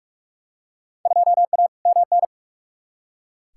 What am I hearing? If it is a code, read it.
Morse code, heard as a keyed tone, about 46 wpm, 700 Hz. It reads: 2A MN
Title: Field Day Exchanges Encore - 45wpm